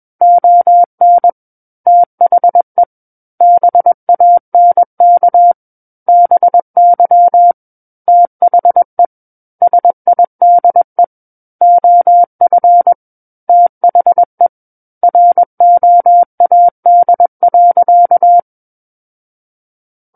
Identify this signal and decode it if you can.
Morse code, heard as a keyed tone, about 21 wpm, 700 Hz. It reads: ON THE BANK BY THE SIDE OF THE ROAD.
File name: Book One / 12-Chapter 12 - 21 WPM